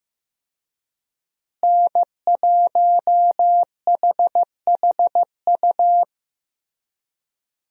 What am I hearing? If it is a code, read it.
Morse code, heard as a keyed tone, about 15 wpm, 700 Hz. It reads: N1HHU
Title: USA Call Signs - 5 Characters - Repeated 3x - 15wpm